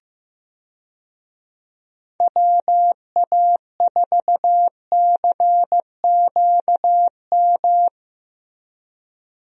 Morse code, heard as a keyed tone, about 16 wpm, 700 Hz. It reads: WA4CQM
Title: USA Call Signs - 6 Characters - Repeated 2x - 15wpm